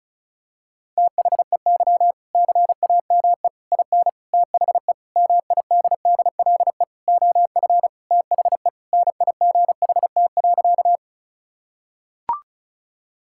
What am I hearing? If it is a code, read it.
Morse code, heard as a keyed tone, about 36 wpm, 700 Hz. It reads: THEY CAME IN THE MIDDLE OF THE NIGHT.
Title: Sentences from Top 1000 Words - Review Part 6 - 35wpm